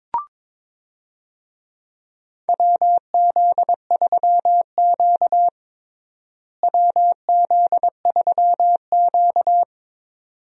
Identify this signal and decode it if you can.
Morse code, heard as a keyed tone, about 22 wpm, 700 Hz. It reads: WZ3Q WZ3Q
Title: USA Call Signs - 4 Characters - Repeated 2x - 22wpm